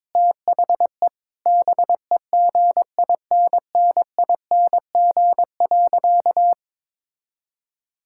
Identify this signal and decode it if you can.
Morse code, heard as a keyed tone, about 22 wpm, 700 Hz. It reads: THE BEGINNING.
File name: practice / 22wpm / Sentences with 4-Word Phrases - 22wpm